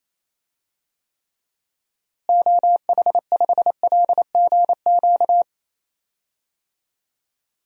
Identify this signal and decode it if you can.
Morse code, heard as a keyed tone, about 28 wpm, 700 Hz. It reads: OH5LGQ
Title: DXCC-100 Call Signs - 6 Characters - Repeated 2x - 28wpm